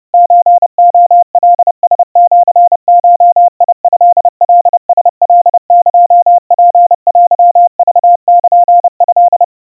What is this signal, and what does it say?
Morse code (Japanese Wabun code), heard as a keyed tone, about 30 wpm, 700 Hz. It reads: ソレカラシゴトカラカエツテクルト
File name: kawa / Morse sample / kamyu 110 cpm 700 Hz Crispy